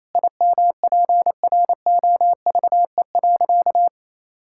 Morse code, heard as a keyed tone, about 28 wpm, 700 Hz. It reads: IMPROVE.
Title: Sentences from Top 1000 Words - Review Part 5 - 28wpm